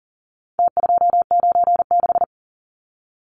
Morse code, heard as a keyed tone, about 40 wpm, 700 Hz. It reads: T296
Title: Numbers - Contest Style - 4-Digit 40wpm